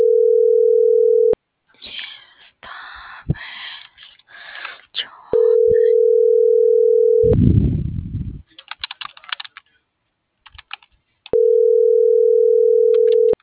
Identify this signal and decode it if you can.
Ringback tone (North America), 440 + 480 Hz.